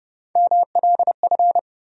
Morse code, heard as a keyed tone, about 30 wpm, 700 Hz. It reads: MLF
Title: USA Call Sign Postfixes - 3 Letters 30wpm